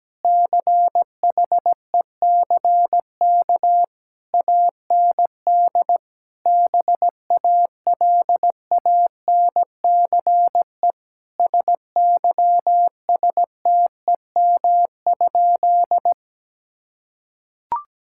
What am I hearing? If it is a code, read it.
Morse code, heard as a keyed tone, about 17 wpm, 700 Hz. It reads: CHECK AND BALANCE SYSTEM?
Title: Sentences from Top 1000 Words - Review Part 6 - 17wpm